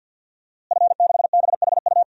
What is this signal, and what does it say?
Morse code, heard as a keyed tone, about 52 wpm, 700 Hz. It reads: F6BHU
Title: DXCC-100 Call Signs - 5 Characters - Repeated 2x - 50wpm